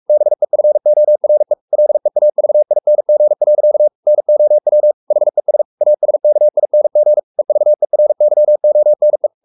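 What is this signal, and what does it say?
Morse code, heard as a keyed tone, about 44 wpm, 600 Hz. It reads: BEFORE LEAVING. NOW HES ASKING EVERYONE
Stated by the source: Morse code at 43 wpm sending Ken Follett's novel Never in English, from CW med Gnister.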